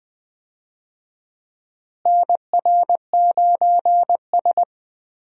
Morse code, heard as a keyed tone, about 20 wpm, 700 Hz. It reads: NR9S
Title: USA Call Signs - 4 Characters - Repeated 2x - 20wpm